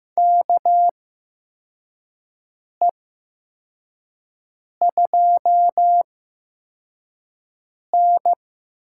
Morse code, heard as a keyed tone, about 15 wpm, spaced as for 4 wpm, 700 Hz. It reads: KE2N
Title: USA Call Signs - 4 Characters - 8x Character Spacing - 15wpm